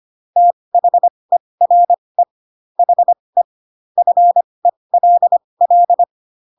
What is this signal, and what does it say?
Morse code, heard as a keyed tone, about 25 wpm, spaced as for 20 wpm, 700 Hz. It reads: THERE HE FELL
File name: Pinocchio / Pinocchio ch6 15 25 ik1hge